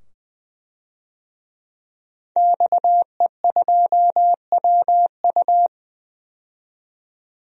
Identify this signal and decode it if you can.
Morse code, heard as a keyed tone, about 20 wpm, 700 Hz. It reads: XE2WU
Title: DXCC-100 Call Signs - 5 Characters - 20wpm